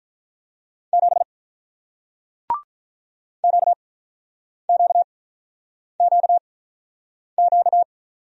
Morse code, heard as a keyed tone, about 48 wpm, spaced as for 9 wpm, 700 Hz. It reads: 7 QQQQ